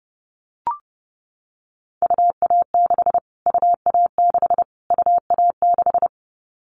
Morse code, heard as a keyed tone, about 30 wpm, 700 Hz. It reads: UA6 UA6 UA6
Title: Callsign Prefixes for Top 25 DXCC - Repeated 3x - 30wpm